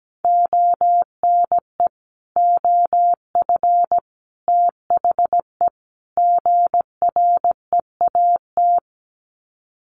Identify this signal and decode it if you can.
Morse code, heard as a keyed tone, about 17 wpm, 700 Hz. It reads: ONE OF THE GREAT
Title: Sets of 4 Words 17wpm